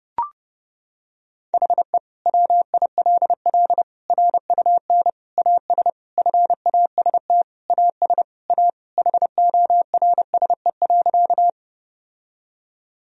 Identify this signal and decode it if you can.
Morse code, heard as a keyed tone, about 30 wpm, 700 Hz. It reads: HE WILL RUN AS FAST AS A HORSE.
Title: Sentences from Top 1000 Words - Review Part 2 - 30wpm